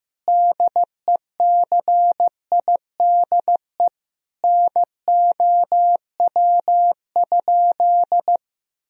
Morse code, heard as a keyed tone, about 15 wpm, 700 Hz. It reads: DECIDE NOW?
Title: Sentences from Top 1000 Words - Review Part 3 - 15wpm